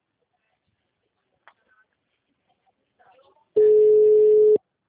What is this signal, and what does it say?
Ringback tone (Europe, 425 Hz), with speech.
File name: out- -2020- - - .1503